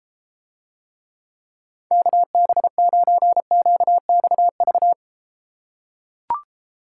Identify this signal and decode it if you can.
Morse code, heard as a keyed tone, about 33 wpm, 700 Hz. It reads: KB9QXV